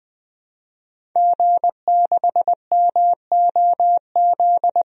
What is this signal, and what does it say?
Morse code, heard as a keyed tone, about 20 wpm, 700 Hz. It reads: G6MOZ
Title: DXCC-100 Call Signs - 5 Characters - 20wpm